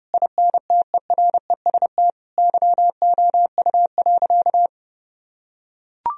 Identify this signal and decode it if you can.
Morse code, heard as a keyed tone, about 30 wpm, 700 Hz. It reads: INTEREST YOU.
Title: Sentences from Top 1000 Words - Review Part 2 - 30wpm